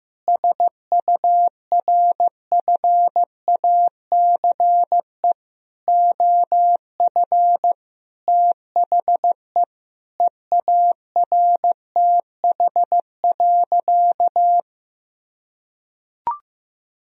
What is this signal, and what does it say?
Morse code, heard as a keyed tone, about 15 wpm, 700 Hz. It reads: SURFACE OF THE EARTH.
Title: Sentences from Top 1000 Words - Review Part 3 - 15wpm